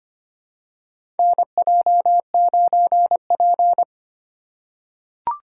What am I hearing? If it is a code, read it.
Morse code, heard as a keyed tone, about 25 wpm, 700 Hz. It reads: NJ9P
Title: USA Call Signs - 6x Character Spacing - 25wpm